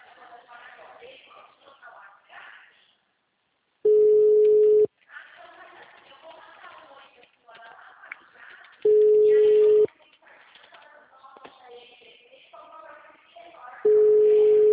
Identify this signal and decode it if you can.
Ringback tone (Europe, 425 Hz), with speech.
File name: out- -2002- - - .51267